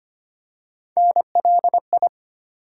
Morse code, heard as a keyed tone, about 25 wpm, 700 Hz. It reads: NLI
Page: morse-code-ninja-practice-25wpm